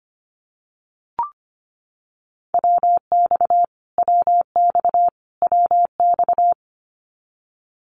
Morse code, heard as a keyed tone, about 25 wpm, 700 Hz. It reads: WX WX WX
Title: Abbreviations Repeated 3x - 25wpm